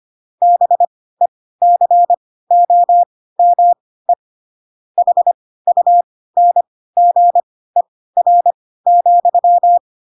Morse code, heard as a keyed tone, about 25 wpm, spaced as for 16 wpm, 700 Hz. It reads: BECOME HUNGER,
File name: Pinocchio / Pinocchio ch5 10 25 ik1hge